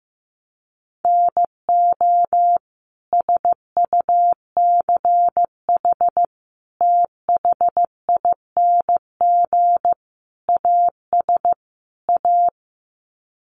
Morse code, heard as a keyed tone, about 15 wpm, 700 Hz. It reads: NO SUCH THING AS A
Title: Sets of 5 Words Encore 15wpm